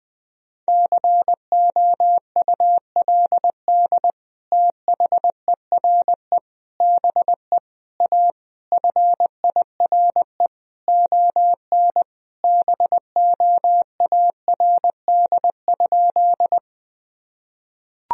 Morse code, heard as a keyed tone, about 20 wpm, 700 Hz. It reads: COULD THERE BE A FIRE ON BOARD?
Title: Sentences from Top 1000 Words - Review Part 4 - 20wpm